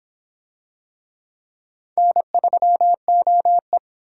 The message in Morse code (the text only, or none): N3OE